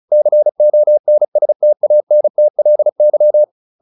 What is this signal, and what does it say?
Morse code, heard as a keyed tone, about 34 wpm, 600 Hz. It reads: CONSTANTLY
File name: the old man and the sea 2 35wpm